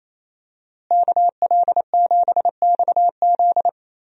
Morse code, heard as a keyed tone, about 28 wpm, 700 Hz. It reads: KL7XZ
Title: DXCC-100 Call Signs - 5 Characters - Repeated 3x - 28wpm